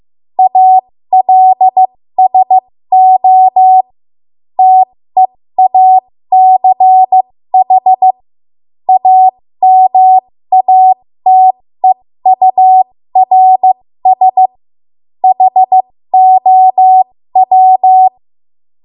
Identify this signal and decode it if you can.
Morse code, heard as a keyed tone, about 15 wpm, spaced as for 13 wpm, 750 Hz. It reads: ALSO TEACH AMATEURS HOW
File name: Morse / 13WPM